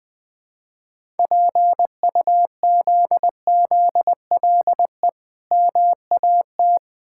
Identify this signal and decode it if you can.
Morse code, heard as a keyed tone, about 20 wpm, 690 Hz. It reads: PUZZLE MAT
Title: Difficult Words with Hints - 20wpm